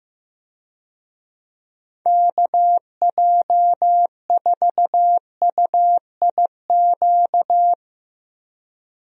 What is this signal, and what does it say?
Morse code, heard as a keyed tone, about 15 wpm, 700 Hz. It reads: KJ4UIQ
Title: USA Call Signs - 6 Characters - Repeated 2x - 15wpm